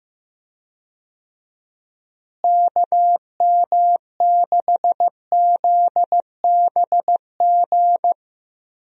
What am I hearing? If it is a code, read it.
Morse code, heard as a keyed tone, about 15 wpm, 700 Hz. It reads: KM6ZBG
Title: USA Call Signs - 6 Characters - Repeated 2x - 15wpm